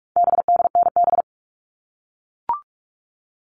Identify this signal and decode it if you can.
Morse code, heard as a keyed tone, about 44 wpm, 700 Hz. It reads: BDNB
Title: All Cut Numbers - 4-Digit 45wpm